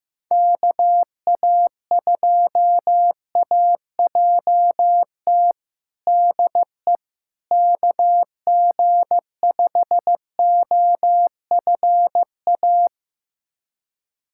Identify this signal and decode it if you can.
Morse code, heard as a keyed tone, about 15 wpm, 700 Hz. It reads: KA2AJT DE KG5OFA
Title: USA Callsign DE Callsign - 6 Characters - 15wpm